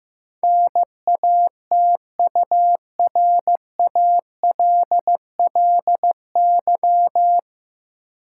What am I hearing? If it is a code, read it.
Morse code, heard as a keyed tone, about 15 wpm, 700 Hz. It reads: NATURALLY
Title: Sentences from Top 1300 Words - New Word Drill - 15wpm